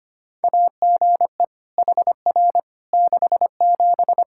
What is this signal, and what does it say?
Morse code, heard as a keyed tone, about 25 wpm, 700 Hz. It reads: AGE HR 67
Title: K1USN SST QSO Practice 25wpm